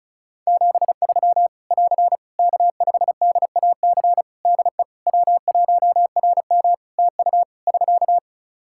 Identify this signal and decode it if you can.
Morse code, heard as a keyed tone, about 36 wpm, 700 Hz. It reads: 73 <AR> K5DAC DE W1RM TU <SK>
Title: Practice QSO - Exchange 4 - Ending - 35wpm